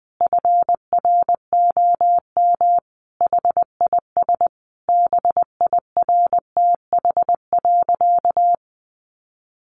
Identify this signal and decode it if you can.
Morse code, heard as a keyed tone, about 20 wpm, 700 Hz. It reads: FROM HIS BIRTH.
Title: Sentences from Top 900 Words 20wpm